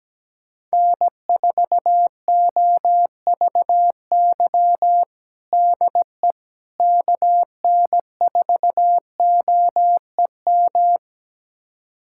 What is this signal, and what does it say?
Morse code, heard as a keyed tone, about 17 wpm, 700 Hz. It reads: N4OVY DE KN4OEM
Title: USA Callsign DE Callsign 17wpm